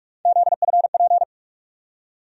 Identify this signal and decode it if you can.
Morse code, heard as a keyed tone, about 46 wpm, 700 Hz. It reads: ZFP
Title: USA Call Sign Postfixes - 3 Letters 45wpm